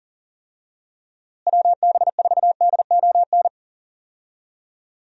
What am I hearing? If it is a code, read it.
Morse code, heard as a keyed tone, about 37 wpm, 700 Hz. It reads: WB4DON
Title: USA Call Signs - 6 Characters - Repeated 2x - 40wpm